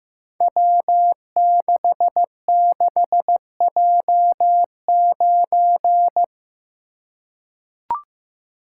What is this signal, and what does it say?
Morse code, heard as a keyed tone, about 15 wpm, 700 Hz. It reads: W66J9
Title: Code Groups - 5 Letter:Numbers - 15wpm